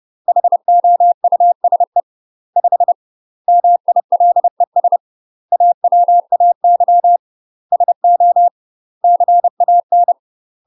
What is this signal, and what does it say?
Morse code, heard as a keyed tone, about 30 wpm, 700 Hz. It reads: HOUSE 5 MILES AWAY SO CAN
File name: UK MASTER QSOs TRAINING / 30WPM / UK MASTER QSOs 0021